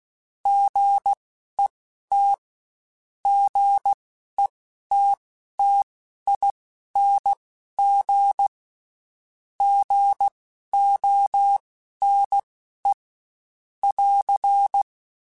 Morse code, heard as a keyed tone, about 16 wpm, spaced as for 12 wpm, 800 Hz. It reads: GET GETTING GONE <AR>